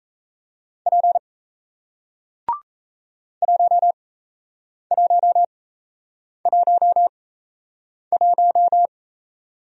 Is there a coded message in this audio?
Yes: Morse code from 0.9 to 8.9 s, heard as a keyed tone, about 37 wpm, spaced as for 8 wpm, 700 Hz. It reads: P 1111